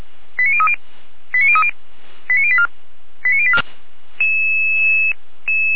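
Five-tone selective call (ZVEI), heard as quick runs of five tones, tones of about 80 ms. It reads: 89030 89030 89084 89084
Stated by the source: ZVEI five-tone call received by SDR.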